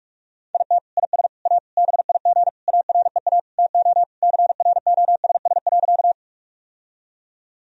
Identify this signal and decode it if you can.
Morse code, heard as a keyed tone, about 46 wpm, 700 Hz. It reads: IT IS A BIG AREA TO CROSS.
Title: Sentences from Top 1000 Words - Review Part 2 - 45wpm